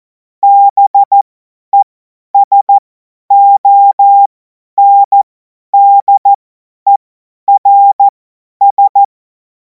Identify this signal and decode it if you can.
Morse code, heard as a keyed tone, about 14 wpm, spaced as for 10 wpm, 800 Hz. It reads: BESONDERS
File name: Morsekurs / Witze / Witz 05. Mai 2026 50 F70